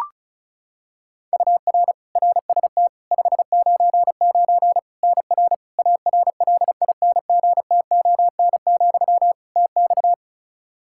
Morse code, heard as a keyed tone, about 36 wpm, 700 Hz. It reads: UR RST 599 NR ARLINGTON, TX
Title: Practice QSO - Exchange 1 - Beginning - 35wpm